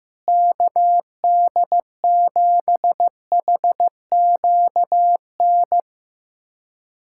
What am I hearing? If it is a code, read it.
Morse code, heard as a keyed tone, about 15 wpm, 700 Hz. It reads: KD7HQN
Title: USA Callsign DE Callsign 15wpm